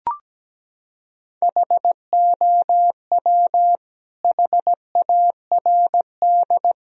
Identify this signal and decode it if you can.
Morse code, heard as a keyed tone, about 17 wpm, 700 Hz. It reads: HOW HARD
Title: Sentences from Top 1000 Words - Review Part 3 - 17wpm